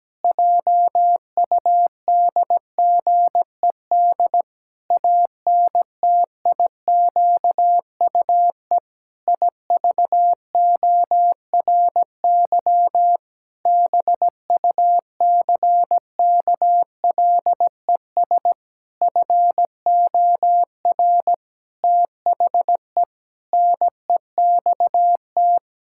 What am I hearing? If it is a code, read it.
Morse code, heard as a keyed tone, about 17 wpm, 700 Hz. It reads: JUDGED ANTIQUE IVORY BUCKLES FOR THE NEXT
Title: Pangram Challenge v2 - 17wpm